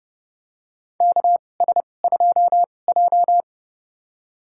Morse code, heard as a keyed tone, about 29 wpm, spaced as for 21 wpm, 700 Hz. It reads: KS2J
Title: USA Call Signs - 4 Characters - 2x Character Spacing - 30wpm